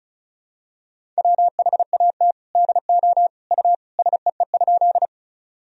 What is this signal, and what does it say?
Morse code, heard as a keyed tone, about 34 wpm, 700 Hz. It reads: WHAT DO U SEE?
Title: Sentences from Top 100 Words and Abbreviations 35wpm